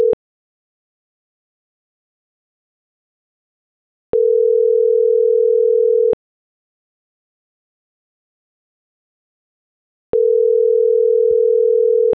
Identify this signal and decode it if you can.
Ringback tone (North America), 440 + 480 Hz.
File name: CENTRAL SAGOT / out adriana.poncec